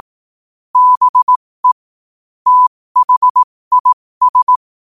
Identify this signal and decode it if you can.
Morse code, heard as a keyed tone, about 18 wpm, spaced as for 15 wpm, 1000 Hz. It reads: BE THIS
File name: qotdfiles / 15wpm